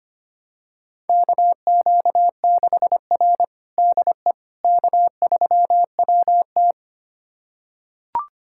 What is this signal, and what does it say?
Morse code, heard as a keyed tone, about 25 wpm, 700 Hz. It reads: KQ6R DE K3WT
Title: USA Callsign DE Callsign - 4 Characters - 25wpm